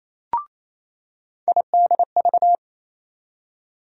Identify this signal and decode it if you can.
Morse code, heard as a keyed tone, about 28 wpm, 700 Hz. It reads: IDV
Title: USA Call Sign Postfixes - 3 Letters 28wpm